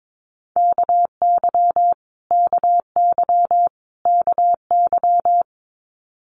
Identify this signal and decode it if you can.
Morse code, heard as a keyed tone, about 22 wpm, 700 Hz. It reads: KY KY KY
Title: Multi-letter Phonemes Repeated 3x 22wpm